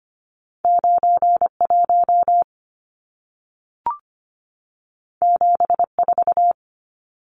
Morse code, heard as a keyed tone, about 25 wpm, 700 Hz. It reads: 91 74
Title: Numbers - 2-Digit 25wpm